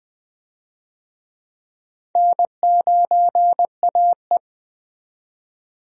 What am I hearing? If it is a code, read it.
Morse code, heard as a keyed tone, about 20 wpm, 700 Hz. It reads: N9AE
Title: USA Call Signs - 4 Characters - 20wpm